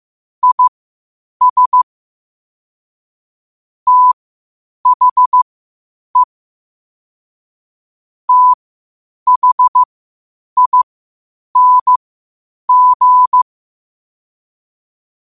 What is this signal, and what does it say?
Morse code, heard as a keyed tone, about 15 wpm, spaced as for 8 wpm, 1000 Hz. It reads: IS THE THING
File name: qotdfiles / 8wpm